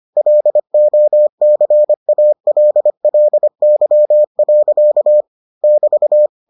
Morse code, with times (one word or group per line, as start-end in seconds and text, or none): 0.2-5.2 LOCALLY.
5.6-6.3 <BT>